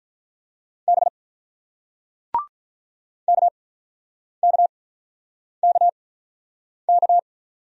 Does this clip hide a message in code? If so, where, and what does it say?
Yes: Morse code from 0.9 to 7.2 s, heard as a keyed tone, about 46 wpm, spaced as for 9 wpm, 700 Hz. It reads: B KKKK